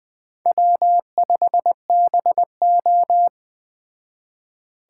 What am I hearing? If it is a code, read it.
Morse code, heard as a keyed tone, about 20 wpm, 700 Hz. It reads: W5BO
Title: Speed Racing - USA 1x2 Call Signs 17wpm